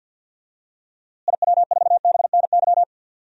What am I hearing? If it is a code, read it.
Morse code, heard as a keyed tone, about 48 wpm, 700 Hz. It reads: IW4BNY
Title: DXCC-100 Call Signs - 6 Characters - Repeated 3x - 50wpm